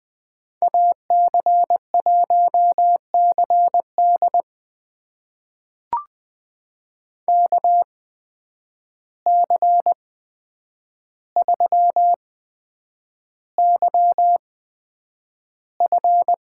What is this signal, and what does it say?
Morse code, heard as a keyed tone, about 20 wpm, spaced as for 9 wpm, 700 Hz. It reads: AC1CD KC3YF
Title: USA Call Signs - 5 Characters - 8x Character Spacing - 20wpm